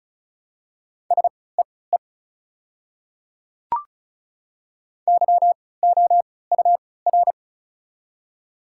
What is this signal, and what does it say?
Morse code, heard as a keyed tone, about 36 wpm, spaced as for 20 wpm, 690 Hz. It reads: SEE YOUR
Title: 3x Character Spacing - Top 100 Words - 35wpm